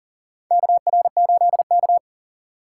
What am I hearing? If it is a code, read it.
Morse code, heard as a keyed tone, about 40 wpm, 700 Hz. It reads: KR8K
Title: USA Call Signs - 4 Characters - 6x Character Spacing - 40wpm